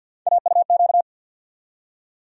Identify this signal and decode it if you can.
Morse code, heard as a keyed tone, about 51 wpm, 700 Hz. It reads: AUQ
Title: USA Call Sign Postfixes - 3 Letters 50wpm